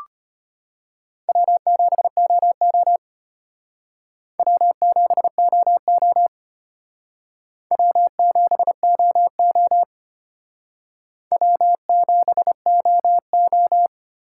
Morse code, heard as a keyed tone, about 32 wpm, 700 Hz. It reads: W7OO W7OO W7OO W7OO